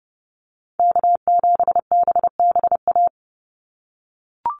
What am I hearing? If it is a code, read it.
Morse code, heard as a keyed tone, about 30 wpm, 700 Hz. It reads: K7BBA
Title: USA Call Signs 30wpm